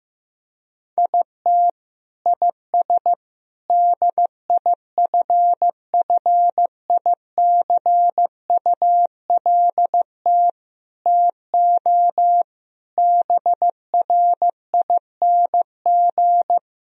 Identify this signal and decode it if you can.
Morse code, heard as a keyed tone, about 15 wpm, 700 Hz. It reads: IT IS DIFFICULT TO BRING
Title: Sentences from Top 1000 Words - Review Part 4 - 15wpm